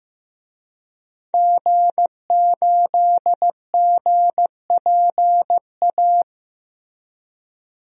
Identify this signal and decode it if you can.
Morse code, heard as a keyed tone, about 15 wpm, 700 Hz. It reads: G8GPA
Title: DXCC-100 Call Signs - 5 Characters - Repeated 2x - 15wpm